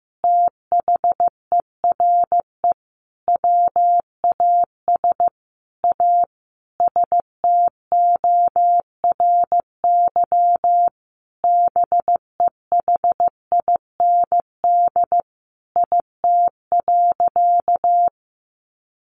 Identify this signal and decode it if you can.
Morse code, heard as a keyed tone, about 15 wpm, 700 Hz. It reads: THERE WAS A STORY BEHIND IT.
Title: Sentences from Top 500 Words 15wpm